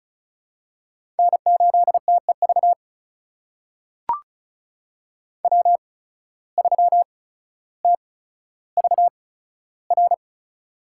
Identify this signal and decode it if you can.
Morse code, heard as a keyed tone, about 36 wpm, spaced as for 15 wpm, 700 Hz. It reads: N8TEV W3TVR
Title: USA Call Signs - 5 Characters - 8x Character Spacing - 35wpm